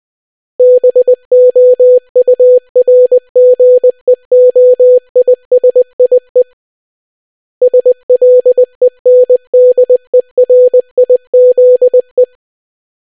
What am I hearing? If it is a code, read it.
Morse code, heard as a keyed tone, about 20 wpm, 500 Hz. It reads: BOURGEOISIE SLENDERIZE